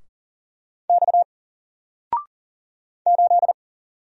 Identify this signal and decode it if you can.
Morse code, heard as a keyed tone, about 40 wpm, 700 Hz. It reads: X 8
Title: Single Character (Letter:Number:Punctuation:Prosign) 40wpm